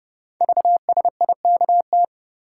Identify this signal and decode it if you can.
Morse code, heard as a keyed tone, about 28 wpm, 700 Hz. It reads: VSIKT
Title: Code Groups - 5 Letters - 30wpm